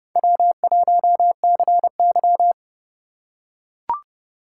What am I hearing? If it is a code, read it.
Morse code, heard as a keyed tone, about 30 wpm, 700 Hz. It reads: W1CY
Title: USA Call Signs - 4 Characters - 3x Character Spacing - 30wpm